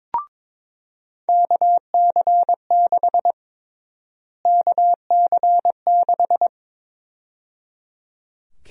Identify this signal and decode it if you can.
Morse code, heard as a keyed tone, about 22 wpm, 700 Hz. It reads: KC6 KC6
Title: USA Call Sign Prefixes - Repeated 2x - 22wpm